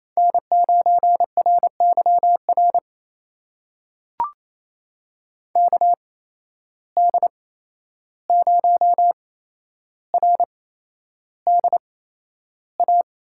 Morse code, heard as a keyed tone, about 27 wpm, spaced as for 11 wpm, 700 Hz. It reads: N9RYR KD0RDA
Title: USA Call Signs - 8x Character Spacing - 28wpm